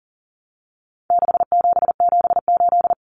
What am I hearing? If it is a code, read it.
Morse code, heard as a keyed tone, about 40 wpm, 700 Hz. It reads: 6778
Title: Numbers - 4-Digit 40wpm